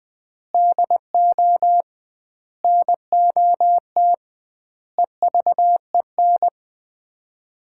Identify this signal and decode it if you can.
Morse code, heard as a keyed tone, about 20 wpm, 700 Hz. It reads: DO NOT EVEN
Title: 2x Word-Spacing - Sets of 4 Words - 20wpm